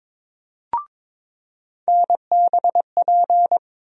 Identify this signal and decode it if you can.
Morse code, heard as a keyed tone, about 22 wpm, 700 Hz. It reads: NBP
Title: USA Call Sign Postfixes - 3 Letters 22wpm